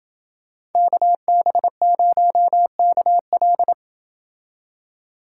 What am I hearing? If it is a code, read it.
Morse code, heard as a keyed tone, about 28 wpm, 700 Hz. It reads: KB0KL